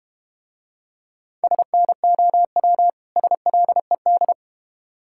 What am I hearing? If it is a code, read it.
Morse code, heard as a keyed tone, about 32 wpm, 700 Hz. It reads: SNOW SLED